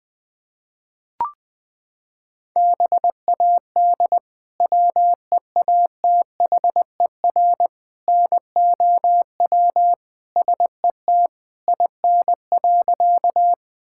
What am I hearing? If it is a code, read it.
Morse code, heard as a keyed tone, about 20 wpm, 700 Hz. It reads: BAD WEATHER NOW SET IN.
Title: Sentences from Top 1000 Words - Review Part 5 - 20wpm